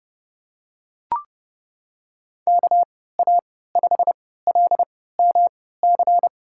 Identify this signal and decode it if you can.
Morse code, heard as a keyed tone, about 30 wpm, spaced as for 17 wpm, 700 Hz. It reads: KA5LMC
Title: USA Call Signs - 6 Characters - 3x Character Spacing - 30wpm